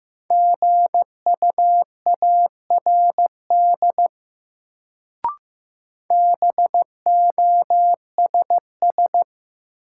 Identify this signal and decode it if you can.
Morse code, heard as a keyed tone, about 15 wpm, 700 Hz. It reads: GUARD BOSS
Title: Sentences from Top 700 Words - New Word Drill - 15wpm